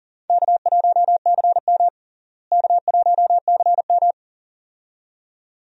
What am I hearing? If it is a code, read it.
Morse code, heard as a keyed tone, about 40 wpm, 700 Hz. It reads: K1CM K1CM
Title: USA Call Signs - 4 Characters - Repeated 2x - 40wpm